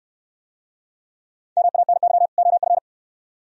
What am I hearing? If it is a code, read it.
Morse code, heard as a keyed tone, about 63 wpm, 700 Hz. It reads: MANY OF